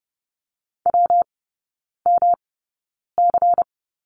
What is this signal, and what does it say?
Morse code, heard as a keyed tone, about 30 wpm, spaced as for 9 wpm, 700 Hz. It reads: WMC
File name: Lesson 19e - Letter w - Three Characters 30wpm